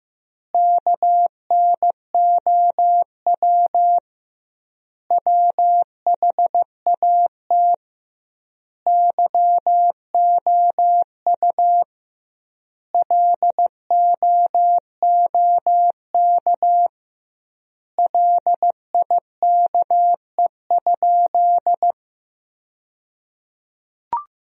Morse code, heard as a keyed tone, about 15 wpm, 700 Hz. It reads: KNOW WHAT YOU LOOK LIKE?
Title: 2x Word-Spacing - Sentences from Top 100 Words - 15wpm